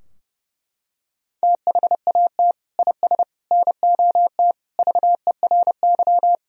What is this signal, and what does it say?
Morse code, heard as a keyed tone, about 32 wpm, 700 Hz. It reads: THAT IS NOT VERY
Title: Sentences from Top 1000 Words - Review Part 4 - 30wpm